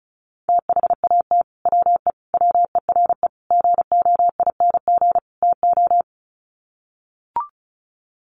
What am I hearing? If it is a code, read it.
Morse code, heard as a keyed tone, about 34 wpm, 700 Hz. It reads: THAT WE WERE GOING TO
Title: Sets of 5 Words Encore2 35wpm